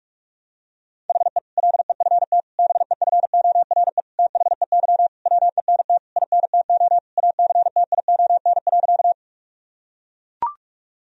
Morse code, heard as a keyed tone, about 41 wpm, 700 Hz. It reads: HE LEFT BEFORE THEY WENT INTO ACTION.
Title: Sentences from Top 1000 Words - Review Part 4 - 45wpm